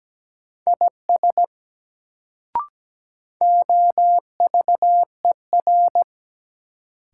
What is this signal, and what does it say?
Morse code, heard as a keyed tone, about 17 wpm, 700 Hz. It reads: IS OVER
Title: Sentences from Top 100 Words - New Word Drill - 17wpm